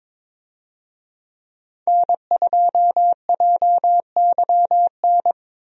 Morse code, heard as a keyed tone, about 22 wpm, 700 Hz. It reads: N2JYN
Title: USA Call Signs - 5 Characters - Repeated 3x - 22wpm